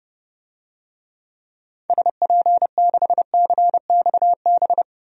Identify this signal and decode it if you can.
Morse code, heard as a keyed tone, about 30 wpm, 700 Hz. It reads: SP6CXB
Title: DXCC-100 Call Signs - 6 Characters - 30wpm